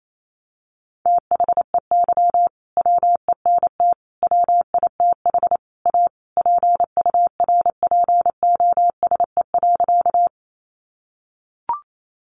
Morse code, heard as a keyed tone, about 28 wpm, 700 Hz. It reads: THEY WENT WITH A PURPOSE.
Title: Sentences from Top 800 Words 28wpm